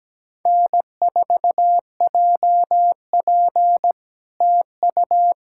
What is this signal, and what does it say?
Morse code, heard as a keyed tone, about 17 wpm, 700 Hz. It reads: N4JP TU
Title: SOTA QSO Practice 17wpm